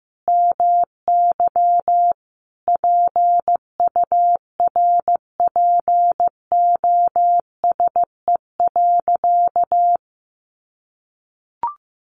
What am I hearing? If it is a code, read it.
Morse code, heard as a keyed tone, about 15 wpm, 700 Hz. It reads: MY PURPOSE.
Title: Sentences from Top 1500 Words 15wpm